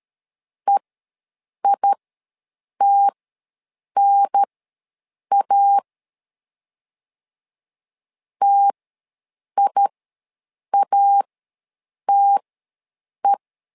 Morse code, heard as a keyed tone, about 13 wpm, spaced as for 7 wpm, 800 Hz. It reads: EITNA TIATE
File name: LEZIONI / 2 UHDN / 2 UHDN ws 12wpm 1708